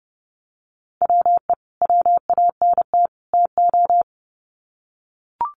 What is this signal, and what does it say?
Morse code, heard as a keyed tone, about 30 wpm, 700 Hz. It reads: WE WANT TO
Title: Sets of 3 Words 30wpm